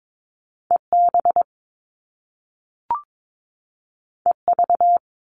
Morse code, heard as a keyed tone, about 22 wpm, 700 Hz. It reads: EB EV
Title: Top Two Letter Combinations in Words 22wpm